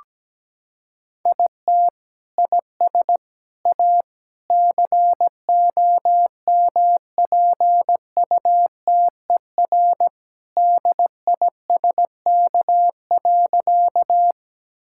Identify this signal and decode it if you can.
Morse code, heard as a keyed tone, about 17 wpm, 700 Hz. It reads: IT IS A COMPUTER DISK.
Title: Sentences from Top 1000 Words - Review Part 6 - 17wpm